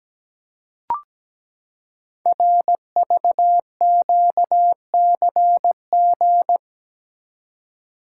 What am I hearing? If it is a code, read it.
Morse code, heard as a keyed tone, about 17 wpm, 700 Hz. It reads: RVQCG
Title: Code Groups - 5 Letters - 17wpm